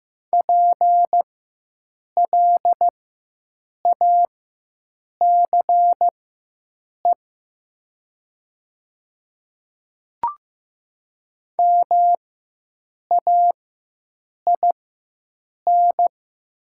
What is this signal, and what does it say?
Morse code, heard as a keyed tone, about 15 wpm, spaced as for 7 wpm, 700 Hz. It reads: PLACE MAIN
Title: 4x Character Spacing - Top 300 Words - 15wpm